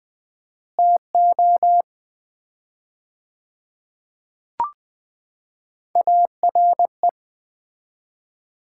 Morse code, heard as a keyed tone, about 20 wpm, 700 Hz. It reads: TO ARE
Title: 4x Word-Spacing - Sets of 5 Words - 20wpm